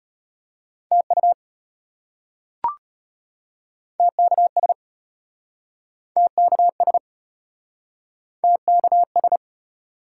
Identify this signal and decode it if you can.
Morse code, heard as a keyed tone, about 35 wpm, 700 Hz. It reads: TU TKS TKS TKS